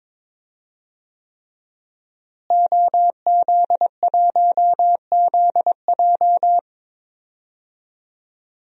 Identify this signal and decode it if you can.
Morse code, heard as a keyed tone, about 22 wpm, 700 Hz. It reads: OZ1ZJ
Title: DXCC-100 Call Signs - 5 Characters - Repeated 3x - 22wpm